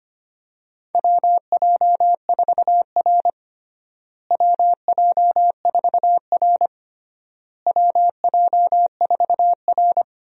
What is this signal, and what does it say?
Morse code, heard as a keyed tone, about 25 wpm, 700 Hz. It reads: WJ4R WJ4R WJ4R
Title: USA Call Signs - 4 Characters - Repeated 3x - 25wpm